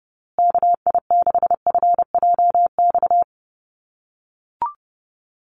Morse code, heard as a keyed tone, about 30 wpm, 700 Hz. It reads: KI6FJX
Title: USA Call Signs Repeated 3x 30wpm